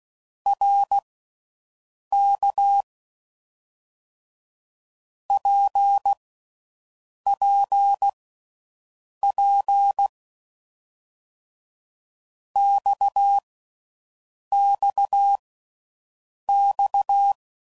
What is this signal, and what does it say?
Morse code, heard as a keyed tone, about 16 wpm, spaced as for 6 wpm, 800 Hz. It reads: RK PPP XXX